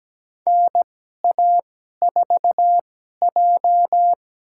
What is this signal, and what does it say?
Morse code, heard as a keyed tone, about 17 wpm, spaced as for 12 wpm, 700 Hz. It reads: NA4J
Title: USA Call Signs - 4 Characters - 2x Character Spacing - 17wpm